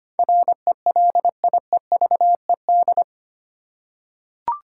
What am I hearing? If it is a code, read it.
Morse code, heard as a keyed tone, about 25 wpm, 700 Hz. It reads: RELIEVED
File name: 8 Letter Words - First 3 Letters Given - 25wpm